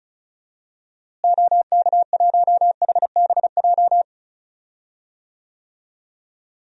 Morse code, heard as a keyed tone, about 36 wpm, 700 Hz. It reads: OK1HBJ
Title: DXCC-100 Call Signs - 6 Characters - Repeated 3x - 35wpm